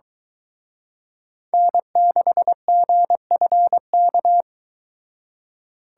Morse code, heard as a keyed tone, about 23 wpm, 700 Hz. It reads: N6GFK